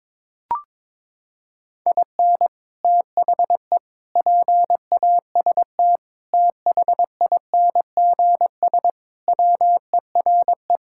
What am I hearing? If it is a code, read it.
Morse code, heard as a keyed tone, about 21 wpm, 700 Hz. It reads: IN THE PAST THINGS WERE
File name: practice / 22wpm / Sentences with 3-Word Phrases - 22wpm